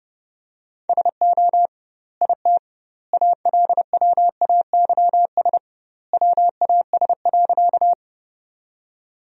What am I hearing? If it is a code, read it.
Morse code, heard as a keyed tone, about 30 wpm, 700 Hz. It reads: SO IT ALWAYS WAS.
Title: 2x Word-Spacing - Sentences from Top 300 Words - 30wpm